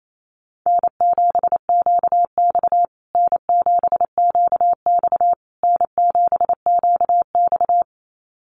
Morse code, heard as a keyed tone, about 28 wpm, 700 Hz. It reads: N7QX N7QX N7QX
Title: USA Call Signs Repeated 3x 28wpm